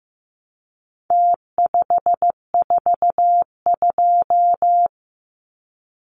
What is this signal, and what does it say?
Morse code, heard as a keyed tone, about 15 wpm, 700 Hz. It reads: T542
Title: Numbers - Contest Style - 4-Digit 15wpm